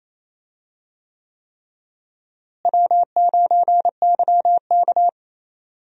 Morse code, heard as a keyed tone, about 28 wpm, 700 Hz. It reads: W9YK
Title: USA Call Signs - 4 Characters - Repeated 3x - 28wpm